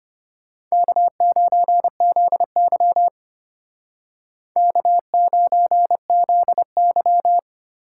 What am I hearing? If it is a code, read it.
Morse code, heard as a keyed tone, about 32 wpm, 700 Hz. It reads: K9ZY K9ZY